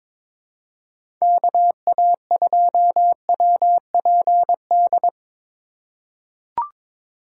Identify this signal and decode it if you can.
Morse code, heard as a keyed tone, about 23 wpm, 700 Hz. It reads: KA2WPD
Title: USA Call Signs - 6 Characters - 4x Character Spacing - 22wpm